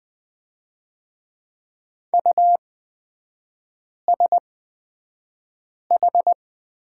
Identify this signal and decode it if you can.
Morse code, heard as a keyed tone, about 20 wpm, spaced as for 5 wpm, 690 Hz. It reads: USH
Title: 229 Lesson 37e - Letter x - Three Characters 20wpm